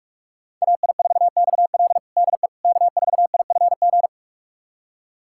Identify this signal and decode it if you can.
Morse code, heard as a keyed tone, about 42 wpm, 700 Hz. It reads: AI4XL DE K4IFG
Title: USA Callsign DE Callsign - 5 Characters - 45wpm